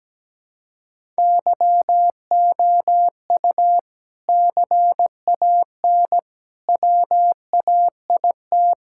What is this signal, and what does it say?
Morse code, heard as a keyed tone, about 17 wpm, 700 Hz. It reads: YOU CAN WAIT
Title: Sentences from Top 1000 Words - Review Part 6 - 17wpm